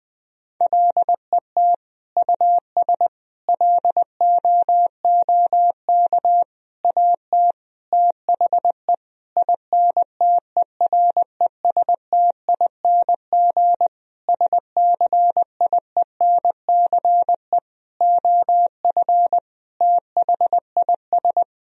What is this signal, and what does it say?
Morse code, heard as a keyed tone, about 20 wpm, 700 Hz. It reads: LET US LOOK AT THE INTERESTING SCIENCE OF THIS
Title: Sentences from Top 1000 Words - Review Part 2 - 20wpm